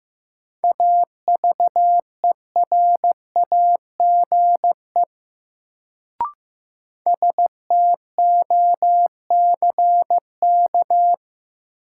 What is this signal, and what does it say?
Morse code, heard as a keyed tone, about 15 wpm, 700 Hz. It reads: AVERAGE STOCK
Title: Sets of 5 Words - New Word Drill - 15wpm